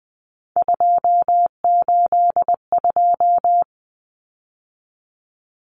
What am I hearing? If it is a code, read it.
Morse code, heard as a keyed tone, about 20 wpm, 700 Hz. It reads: 282
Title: Numbers - 3-Digit 20wpm